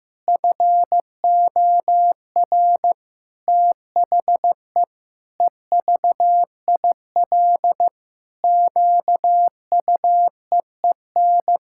Morse code, heard as a keyed tone, about 16 wpm, 700 Hz. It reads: FOR THE EVIL QUEEN
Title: Pangram Challenge v2 - 15wpm